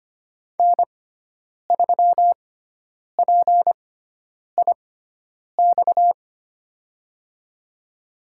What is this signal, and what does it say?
Morse code, heard as a keyed tone, about 25 wpm, spaced as for 9 wpm, 700 Hz. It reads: N3PIX
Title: USA Call Signs - 5 Characters - 6x Character Spacing - 25wpm